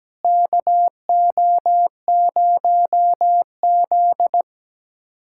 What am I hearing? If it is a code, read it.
Morse code, heard as a keyed tone, about 17 wpm, 700 Hz. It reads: KO0Z
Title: K1USN SST QSO Practice - 17wpm